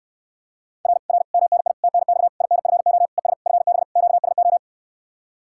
Morse code, heard as a keyed tone, about 69 wpm, 700 Hz. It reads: R R MNI TNX INFO ES FB QSO